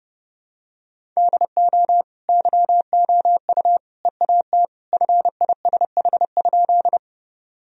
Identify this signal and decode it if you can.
Morse code, heard as a keyed tone, about 30 wpm, 700 Hz. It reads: DO YOU EAT FISH?
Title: Sentences from Top 1000 Words - Review Part 1 - 30wpm